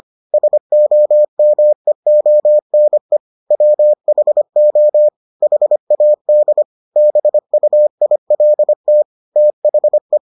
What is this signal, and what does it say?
Morse code, heard as a keyed tone, about 25 wpm, 600 Hz. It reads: SOMEONE WHO HAD BUILT THE